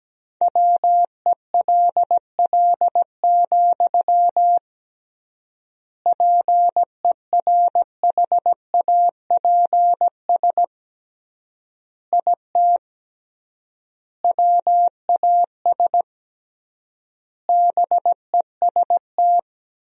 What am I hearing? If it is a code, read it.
Morse code, heard as a keyed tone, about 17 wpm, 700 Hz. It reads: WELL, PERHAPS IT WAS BEST
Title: 3x Word-Spacing - Sentences from Top 500 Words - 17wpm